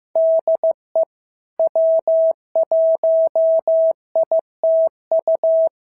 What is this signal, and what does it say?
Morse code, heard as a keyed tone, about 16 wpm, 650 Hz. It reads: DE W1ITU
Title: SOTA QSO Practice 15wpm